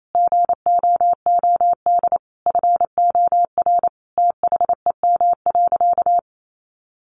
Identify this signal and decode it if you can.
Morse code, heard as a keyed tone, about 28 wpm, 700 Hz. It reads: GOOD FOR THEM.
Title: Sentences from Top 1300 Words 28wpm